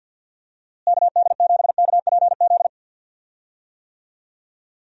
Morse code, heard as a keyed tone, about 47 wpm, 700 Hz. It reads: KD7KPZ
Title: USA Call Signs - 6 Characters - Repeated 2x - 50wpm